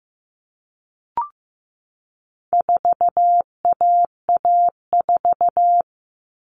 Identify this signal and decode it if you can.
Morse code, heard as a keyed tone, about 15 wpm, 700 Hz. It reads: 4AA4
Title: All Cut Numbers - 4-Digit 15wpm